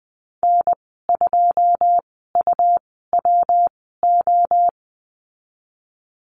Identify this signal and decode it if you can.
Morse code, heard as a keyed tone, about 20 wpm, spaced as for 15 wpm, 700 Hz. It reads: N2UWO
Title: 122 Lesson 20g - Number 3 - Call Signs 20wpm